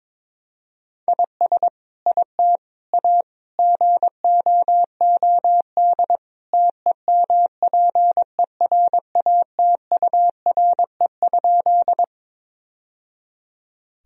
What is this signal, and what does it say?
Morse code, heard as a keyed tone, about 22 wpm, 700 Hz. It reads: IS IT A GOOD TEMPERATURE?
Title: Sentences from Top 1000 Words - Review Part 4 - 22wpm